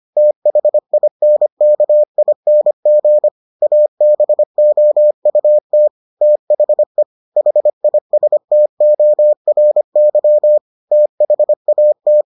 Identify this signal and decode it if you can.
Morse code, heard as a keyed tone, about 25 wpm, 600 Hz. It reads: THINKING ABOUT THE HISTORY THAT